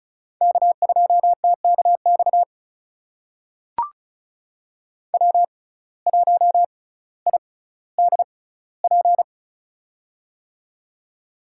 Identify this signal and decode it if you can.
Morse code, heard as a keyed tone, about 36 wpm, spaced as for 18 wpm, 700 Hz. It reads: K2TKX W1IDP
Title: USA Call Signs - 5 Characters - 6x Character Spacing - 35wpm